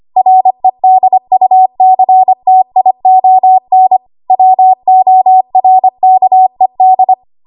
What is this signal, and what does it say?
Morse code, heard as a keyed tone, about 25 wpm, 750 Hz. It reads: REDUCTION WORKED
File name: Archive / 25 WPM / 25WPM